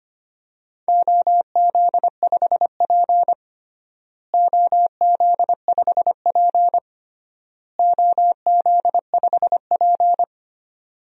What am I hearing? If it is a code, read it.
Morse code, heard as a keyed tone, about 25 wpm, 700 Hz. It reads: OZ5P OZ5P OZ5P
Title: DXCC-100 Call Signs - 4 Characters - Repeated 3x - 25wpm